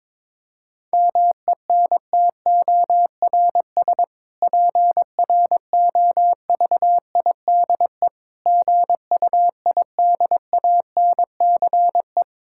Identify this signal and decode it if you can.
Morse code, heard as a keyed tone, about 22 wpm, 700 Hz. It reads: MENTORS PROVIDE GUIDANCE
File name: Difficult Sentences with Hints - 22wpm